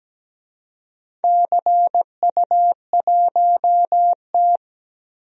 Morse code, heard as a keyed tone, about 17 wpm, 700 Hz. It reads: CU1T
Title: DXCC-100 Call Signs - 4 Characters - Repeated 3x - 17wpm